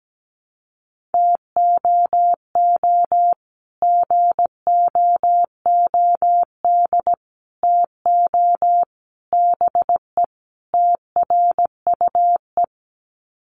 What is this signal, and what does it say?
Morse code, heard as a keyed tone, about 17 wpm, 700 Hz. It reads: TOO GOOD TO BE TRUE
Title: Sets of 5 Words Encore 17wpm